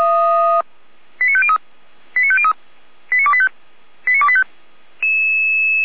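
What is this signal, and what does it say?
Five-tone selective call (ZVEI), heard as quick runs of five tones, tones of about 70 ms. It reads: 89503 89503 89285 89285
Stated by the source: ZVEI five-tone selective call 89270 received by SDR.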